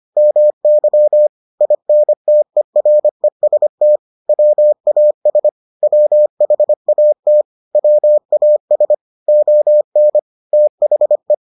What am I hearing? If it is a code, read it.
Morse code, heard as a keyed tone, about 25 wpm, 600 Hz. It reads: MY INTEREST WAS WHAT WAS ON THE